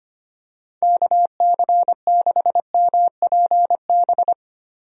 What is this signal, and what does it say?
Morse code, heard as a keyed tone, about 25 wpm, 700 Hz. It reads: KC6MPB
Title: USA Call Signs - 6 Characters - Repeated 3x - 25wpm